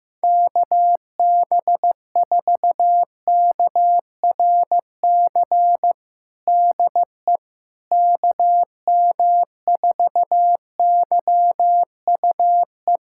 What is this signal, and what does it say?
Morse code, heard as a keyed tone, about 15 wpm, 700 Hz. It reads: KB4KRC DE KM4YUE
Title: USA Callsign DE Callsign - 6 Characters - 15wpm